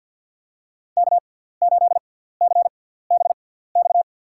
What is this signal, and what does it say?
Morse code, heard as a keyed tone, about 48 wpm, spaced as for 17 wpm, 700 Hz. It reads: K8CBX
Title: USA Call Signs - 5 Characters - 6x Character Spacing - 50wpm